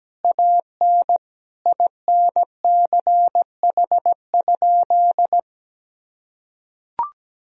Morse code, heard as a keyed tone, about 17 wpm, 700 Hz. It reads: AN INCH?
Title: Sentences from Top 1000 Words - Review Part 3 - 17wpm